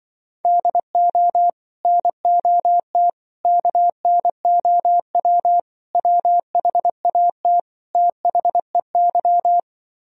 Morse code, heard as a keyed tone, about 24 wpm, 700 Hz. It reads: DO NOT KNOW WHAT THEY
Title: Speed-Racing - Sets of 5 Words 20wpm